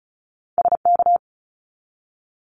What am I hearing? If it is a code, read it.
Morse code, heard as a keyed tone, about 34 wpm, 700 Hz. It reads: SK
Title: ARRL Sections 35wpm